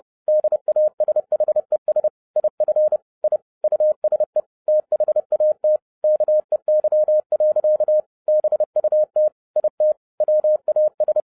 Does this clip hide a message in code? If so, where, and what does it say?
Yes: Morse code from 0.3 to 11.2 s, heard as a keyed tone, about 30 wpm, 610 Hz. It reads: DASHES IF I USE THAT KEY. BUT IT WAS